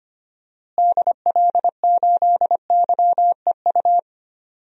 Morse code, heard as a keyed tone, about 25 wpm, 700 Hz. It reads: DL8YEU
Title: DXCC-100 Call Signs - 6 Characters - Repeated 2x - 25wpm